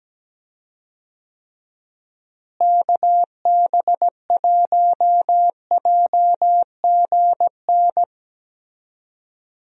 Morse code, heard as a keyed tone, about 17 wpm, 700 Hz. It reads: KB1JGN
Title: USA Call Signs - 6 Characters - Repeated 3x - 17wpm